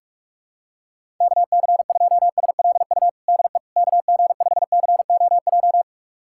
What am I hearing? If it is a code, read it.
Morse code, heard as a keyed tone, about 46 wpm, 700 Hz. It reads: KC2SLU DE KG5COJ
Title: USA Callsign DE Callsign - 6 Characters - 45wpm